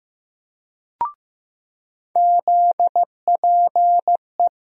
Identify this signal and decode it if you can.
Morse code, heard as a keyed tone, about 15 wpm, 700 Hz. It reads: ZPE
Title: USA Call Sign Postfixes - 3 Letters 15wpm